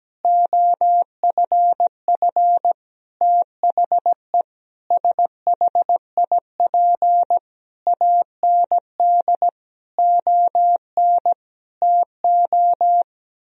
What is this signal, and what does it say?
Morse code, heard as a keyed tone, about 17 wpm, 700 Hz. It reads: OFF THE SHIP AND ON TO
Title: Sentences from Top 1000 Words - Review Part 2 - 17wpm